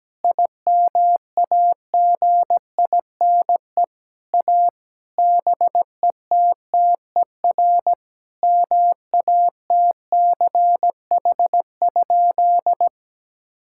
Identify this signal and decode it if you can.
Morse code, heard as a keyed tone, about 17 wpm, 700 Hz. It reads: IMAGINE A BETTER MATCH?
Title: Sentences from Top 1600 Words 17wpm